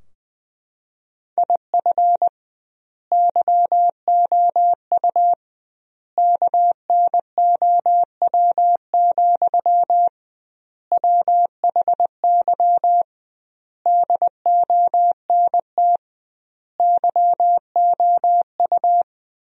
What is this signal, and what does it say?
Morse code, heard as a keyed tone, about 21 wpm, 700 Hz. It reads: IF YOU KNOW, WHY DONT YOU
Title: 2x Word-Spacing - Sentences from Top 300 Words - 20wpm